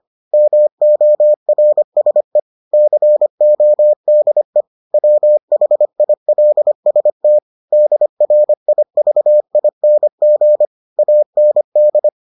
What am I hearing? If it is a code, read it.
Morse code, heard as a keyed tone, about 25 wpm, 600 Hz. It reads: MORSE CODE WHILST DRIVING AND